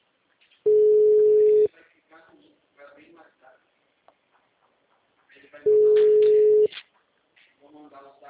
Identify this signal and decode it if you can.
Ringback tone (Europe, 425 Hz), with speech.
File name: out- -2041- - - .89302